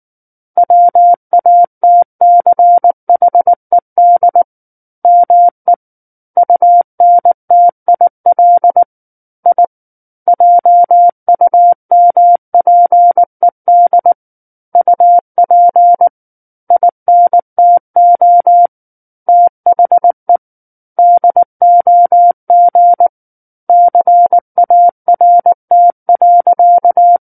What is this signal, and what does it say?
Morse code, heard as a keyed tone, about 19 wpm, 700 Hz. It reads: WATCHED ME UNTIL I JUMPED UP INTO THE DOG CART.
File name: Book One / 10-Chapter 10 - 19 WPM